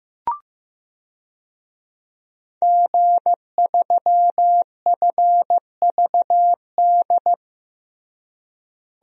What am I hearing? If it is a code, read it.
Morse code, heard as a keyed tone, about 15 wpm, 700 Hz. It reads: G3FVD
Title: DXCC-100 Call Signs - 5 Characters - Repeated 3x - 15wpm